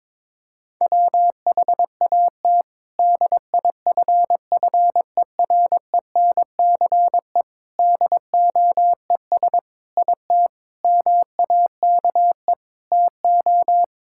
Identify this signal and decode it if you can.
Morse code, heard as a keyed tone, about 22 wpm, 700 Hz. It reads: WHAT DIFFERENCE DOES IT MAKE TO
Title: Sentences from Top 1000 Words - Review Part 5 - 22wpm